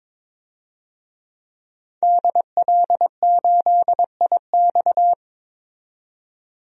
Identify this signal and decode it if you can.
Morse code, heard as a keyed tone, about 22 wpm, 700 Hz. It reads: DL8IX
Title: DXCC-100 Call Signs - 5 Characters - 22wpm